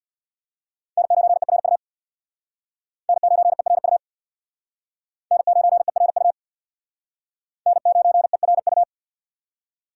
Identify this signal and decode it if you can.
Morse code, heard as a keyed tone, about 63 wpm, 700 Hz. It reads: N9ERU N9ERU N9ERU N9ERU